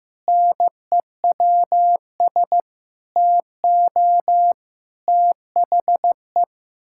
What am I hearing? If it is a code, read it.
Morse code, heard as a keyed tone, about 15 wpm, 700 Hz. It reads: NEWS TO THE
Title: Pangram Challenge v2 - 15wpm